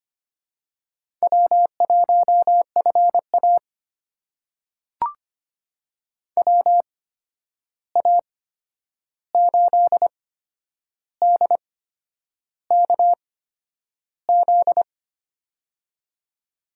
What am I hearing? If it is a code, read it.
Morse code, heard as a keyed tone, about 25 wpm, spaced as for 9 wpm, 700 Hz. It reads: W1FA WA8DKZ
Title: USA Call Signs - 8x Character Spacing - 25wpm